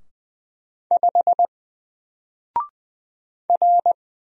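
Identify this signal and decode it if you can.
Morse code, heard as a keyed tone, about 20 wpm, 700 Hz. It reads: 5 R
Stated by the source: Single letter and number Morse drill at 20 wpm, from N4LCD.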